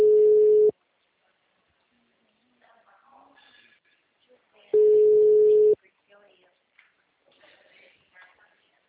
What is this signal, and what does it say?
Ringback tone (Europe, 425 Hz), with speech.